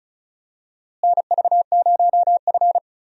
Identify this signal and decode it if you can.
Morse code, heard as a keyed tone, about 36 wpm, 700 Hz. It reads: NV0F
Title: USA Call Signs - 4 Characters - Repeated 3x - 35wpm